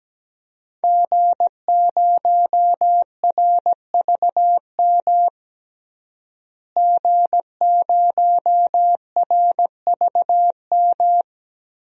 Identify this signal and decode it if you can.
Morse code, heard as a keyed tone, about 17 wpm, 700 Hz. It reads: G0RVM G0RVM
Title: DXCC-100 Call Signs - 5 Characters - Repeated 3x - 17wpm